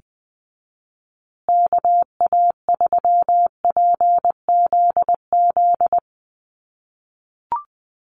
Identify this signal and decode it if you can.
Morse code, heard as a keyed tone, about 20 wpm, 700 Hz. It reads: KA3PZZ
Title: USA Call Signs Repeated 2x 20wpm